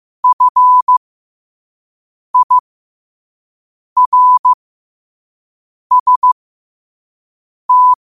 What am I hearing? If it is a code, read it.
Morse code, heard as a keyed tone, about 15 wpm, spaced as for 5 wpm, 1000 Hz. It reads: FIRST
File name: qotdfiles / 5wpm